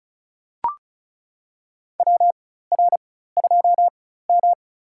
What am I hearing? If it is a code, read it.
Morse code, heard as a keyed tone, about 36 wpm, spaced as for 16 wpm, 700 Hz. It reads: WR2M
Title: USA Call Signs - 4x Character Spacing - 35wpm